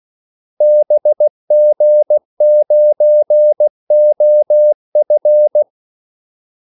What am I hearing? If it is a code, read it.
Morse code, heard as a keyed tone, about 16 wpm, 600 Hz. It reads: BG9OF
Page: Kaldesignaler 16 wpm